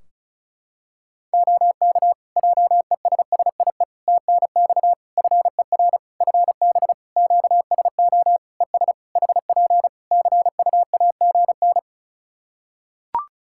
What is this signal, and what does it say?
Morse code, heard as a keyed tone, about 34 wpm, 700 Hz. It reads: OK JESSIE TNX FER FB QSO ES HP CUAGN
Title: Practice QSO - Exchange 4 - Ending - 35wpm